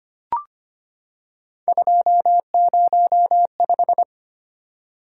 Morse code, heard as a keyed tone, about 25 wpm, 700 Hz. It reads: 205
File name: Three Letter-Number Combinations - 25wpm